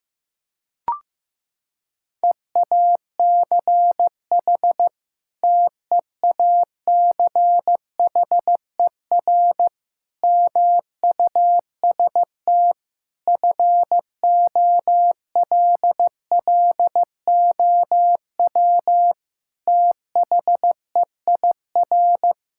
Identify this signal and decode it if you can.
Morse code, heard as a keyed tone, about 14 wpm, 700 Hz. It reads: EACH TEACHER MUST FOLLOW THEIR
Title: Sentences from Top 1000 Words - Review Part 6 - 15wpm